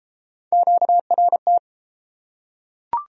Morse code, heard as a keyed tone, about 32 wpm, 700 Hz. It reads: QRT